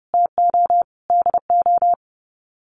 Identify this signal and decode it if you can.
Morse code, heard as a keyed tone, about 30 wpm, 700 Hz. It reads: TO DO
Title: Sets of 5 Words Encore 30wpm